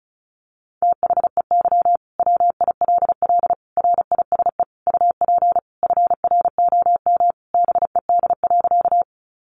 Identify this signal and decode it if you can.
Morse code, heard as a keyed tone, about 34 wpm, 700 Hz. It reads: THEY WILL RISE UP FROM BED.
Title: Sentences from Top 900 Words 35wpm